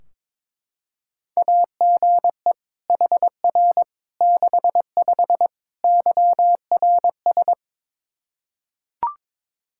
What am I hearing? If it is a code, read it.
Morse code, heard as a keyed tone, about 22 wpm, 700 Hz. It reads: AGE HR 65 YRS
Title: Practice QSO - Exchange 3 - More Info - 22wpm